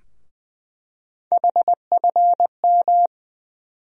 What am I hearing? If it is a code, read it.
Morse code, heard as a keyed tone, about 20 wpm, 700 Hz. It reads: HFM
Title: USA Call Sign Postfixes - 3 Letters 20wpm